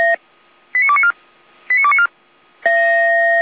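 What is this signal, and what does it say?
Five-tone selective call (ZVEI), heard as quick runs of five tones, tones of about 80 ms. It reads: 89294 89294
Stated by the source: ZVEI five-tone selective call received on an SDR.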